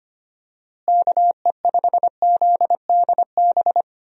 Morse code, heard as a keyed tone, about 25 wpm, 700 Hz. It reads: KE5ZDB
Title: USA Call Signs - 6 Characters - Repeated 2x - 25wpm